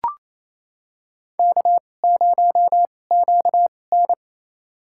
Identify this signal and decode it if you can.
Morse code, heard as a keyed tone, about 28 wpm, spaced as for 20 wpm, 700 Hz. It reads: K0QN
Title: USA Call Signs - 4 Characters - 2x Character Spacing - 28wpm